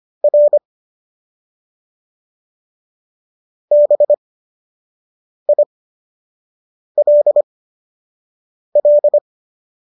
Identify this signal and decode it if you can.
Morse code, heard as a keyed tone, about 25 wpm, spaced as for 6 wpm, 590 Hz. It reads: R BILL